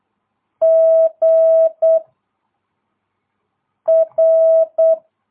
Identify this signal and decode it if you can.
Morse code, heard as a keyed tone, about 8 wpm, 660 Hz. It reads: G R
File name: jh8cbh / 2022cw